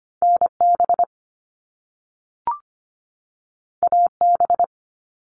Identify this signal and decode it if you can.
Morse code, heard as a keyed tone, about 25 wpm, 700 Hz. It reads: NB AB
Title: Canadian Provinces and Territories 25wpm